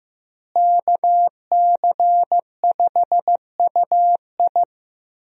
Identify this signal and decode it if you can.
Morse code, heard as a keyed tone, about 15 wpm, 700 Hz. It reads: KC5UI
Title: USA Call Signs - 5 Characters - Repeated 3x - 15wpm